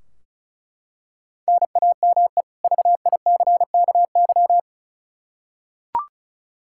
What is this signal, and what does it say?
Morse code, heard as a keyed tone, about 36 wpm, 700 Hz. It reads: NAME VICKY
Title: Practice QSO - Exchange 1 - Beginning - 35wpm